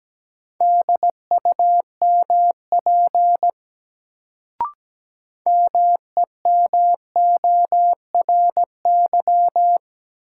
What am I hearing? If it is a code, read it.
Morse code, heard as a keyed tone, about 17 wpm, 700 Hz. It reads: DUMP MEMORY
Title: Sentences from Top 1300 Words - New Word Drill - 17wpm